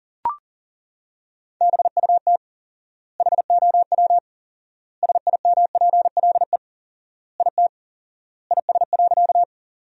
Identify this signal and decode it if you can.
Morse code, heard as a keyed tone, about 40 wpm, 700 Hz. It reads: BUT HOW SIMPLE IT IS.
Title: 4x Word-Spacing - Sentences from Top 400 Words - 40wpm